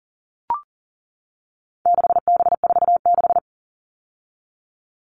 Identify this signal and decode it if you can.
Morse code, heard as a keyed tone, about 40 wpm, 700 Hz. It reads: 6B46
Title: All Cut Numbers - 4-Digit 40wpm